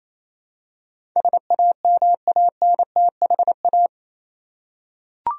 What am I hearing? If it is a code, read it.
Morse code, heard as a keyed tone, about 28 wpm, 700 Hz. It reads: SAMANTHA
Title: USA Common Names - 28wpm